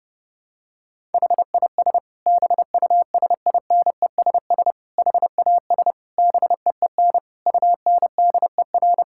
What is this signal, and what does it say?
Morse code, heard as a keyed tone, about 31 wpm, 700 Hz. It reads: HIS BUSINESS HAS BEEN UNDER
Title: Sentences from Top 1000 Words - Review Part 6 - 30wpm